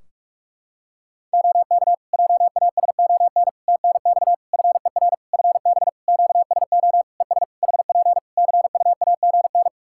Morse code, heard as a keyed tone, about 42 wpm, 700 Hz. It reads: OK JASON TNX FER FB QSO ES HP CUAGN
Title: Practice QSO - Exchange 4 - Ending - 45wpm